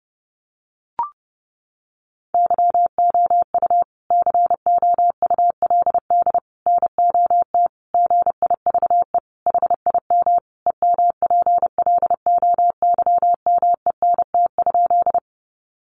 Morse code, heard as a keyed tone, about 30 wpm, 700 Hz. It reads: YOU COULD NOT GIVE HIM EMPLOYMENT?
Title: Sentences from Top 1500 Words 30wpm